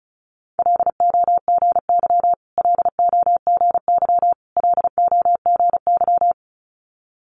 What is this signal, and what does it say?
Morse code, heard as a keyed tone, about 36 wpm, 700 Hz. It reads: LOGY LOGY LOGY
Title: Multi-letter Phonemes Repeated 3x 35wpm